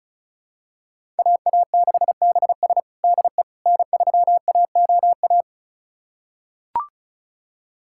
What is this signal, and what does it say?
Morse code, heard as a keyed tone, about 36 wpm, 700 Hz. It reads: AA6BS DE N3AOA
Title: USA Callsign DE Callsign - 5 Characters - 35wpm